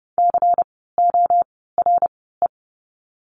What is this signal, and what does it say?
Morse code, heard as a keyed tone, about 30 wpm, spaced as for 17 wpm, 700 Hz. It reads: CORE
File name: Lesson 16f - Letter c - Words 30wpm